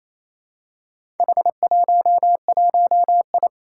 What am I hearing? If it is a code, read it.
Morse code, heard as a keyed tone, about 27 wpm, 700 Hz. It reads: H11I
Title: Four Challenging Letter-Number Combinations 28wpm